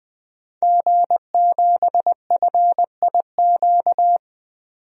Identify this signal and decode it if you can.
Morse code, heard as a keyed tone, about 20 wpm, 700 Hz. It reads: G7FIQ
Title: DXCC-100 Call Signs - 5 Characters - 20wpm